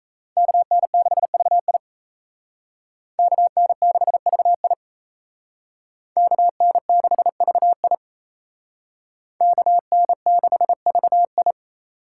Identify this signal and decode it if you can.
Morse code, heard as a keyed tone, about 35 wpm, 700 Hz. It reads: KN6VI KN6VI KN6VI KN6VI